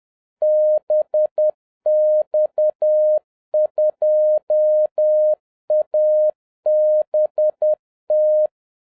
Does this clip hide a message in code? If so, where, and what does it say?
Yes: Morse code from 0.4 to 8.5 s, heard as a keyed tone, about 10 wpm, 610 Hz. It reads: BX2ABT